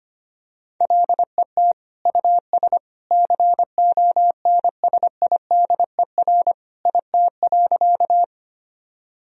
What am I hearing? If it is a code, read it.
Morse code, heard as a keyed tone, about 25 wpm, 700 Hz. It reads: LET US CONSIDER IT.
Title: Sentences from Top 1000 Words - Review Part 4 - 25wpm